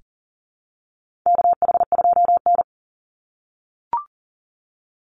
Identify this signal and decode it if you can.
Morse code, heard as a keyed tone, about 40 wpm, 700 Hz. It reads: KH2N
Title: Call Signs - 40wpm